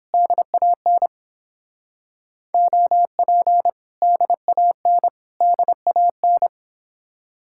Morse code, heard as a keyed tone, about 31 wpm, 700 Hz. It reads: DAN OP DAN DAN